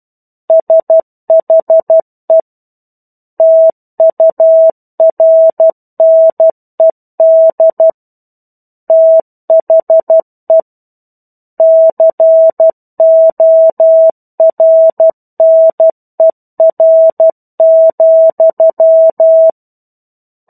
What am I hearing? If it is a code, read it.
Morse code, heard as a keyed tone, about 12 wpm, 650 Hz. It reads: SHE TURNED THE CORNER,
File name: 01-Chapter-01 12 WPM